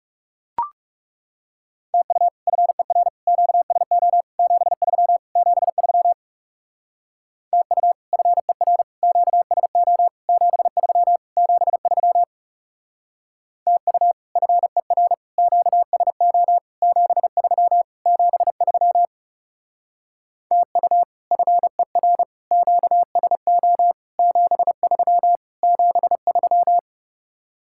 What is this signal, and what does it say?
Morse code, heard as a keyed tone, about 37 wpm, 700 Hz. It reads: TU FER QSO 73 73 TU FER QSO 73 73 TU FER QSO 73 73 TU FER QSO 73 73
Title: Speed-Racing Common QSO Phrases - 30wpm